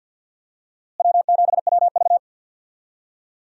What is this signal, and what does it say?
Morse code, heard as a keyed tone, about 46 wpm, 700 Hz. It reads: W7WV
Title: USA Call Signs - 4 Characters - Repeated 3x - 50wpm